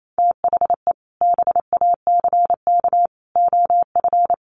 Morse code, heard as a keyed tone, about 28 wpm, 700 Hz. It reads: THE BACK OF
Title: Sets of 3 Words 28wpm